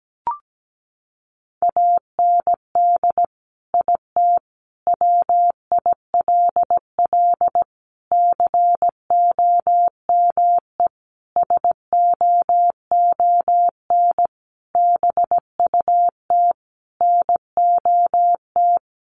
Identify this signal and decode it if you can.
Morse code, heard as a keyed tone, about 17 wpm, 700 Hz. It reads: AND IT WILL COME SOON BUT NOT
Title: Sentences from Top 1100 Words 17wpm